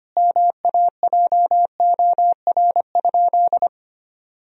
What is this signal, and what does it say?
Morse code, heard as a keyed tone, about 25 wpm, 700 Hz. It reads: MAJOR?
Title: Sentences from Top 1000 Words - Review Part 4 - 25wpm